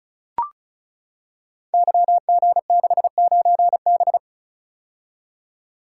Morse code, heard as a keyed tone, about 34 wpm, 700 Hz. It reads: YG69B
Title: Code Groups - 5 Letter:Numbers - 35wpm